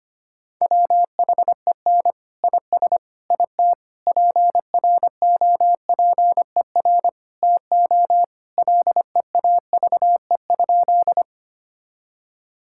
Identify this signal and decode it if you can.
Morse code, heard as a keyed tone, about 25 wpm, 700 Hz. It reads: WHEN IS IT PROPER TO LEAVE?
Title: Sentences from Top 1000 Words - Review Part 5 - 25wpm